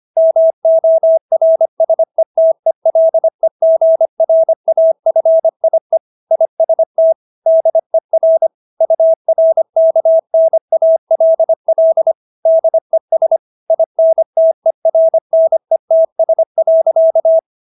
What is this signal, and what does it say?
Morse code, heard as a keyed tone, about 25 wpm, 650 Hz. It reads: MORSETELEGRAFIE IST DER URKNALL DES INTERNETS.
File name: dk5ke / Motto 125BpM